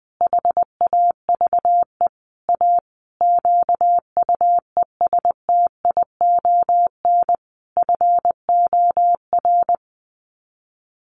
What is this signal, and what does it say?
Morse code, heard as a keyed tone, about 20 wpm, 700 Hz. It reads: HAVE A QUESTION FOR
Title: Sets of 5 Words Encore2 20wpm